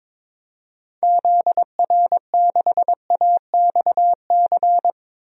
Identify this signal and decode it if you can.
Morse code, heard as a keyed tone, about 22 wpm, 700 Hz. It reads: ZR6AXC
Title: DXCC-100 Call Signs - 6 Characters - 22wpm